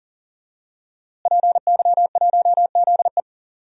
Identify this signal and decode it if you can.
Morse code, heard as a keyed tone, about 38 wpm, 700 Hz. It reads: PY1ZE
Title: DXCC-100 Call Signs - 5 Characters - Repeated 3x - 40wpm